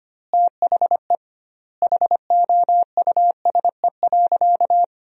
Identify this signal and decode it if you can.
Morse code, heard as a keyed tone, about 25 wpm, 700 Hz. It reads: THE HOUSE.
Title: 2x Word-Spacing - Sentences from Top 400 Words - 25wpm